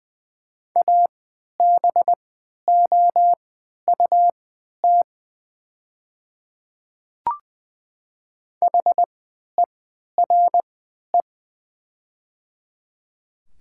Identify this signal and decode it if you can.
Morse code, heard as a keyed tone, about 21 wpm, spaced as for 11 wpm, 690 Hz. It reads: ABOUT HERE
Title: 3x Character Spacing - Top 500 Words - 20wpm